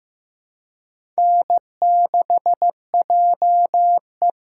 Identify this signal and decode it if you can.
Morse code, heard as a keyed tone, about 15 wpm, 700 Hz. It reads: N6JE
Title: USA Call Signs - 4 Characters - 3x Character Spacing - 15wpm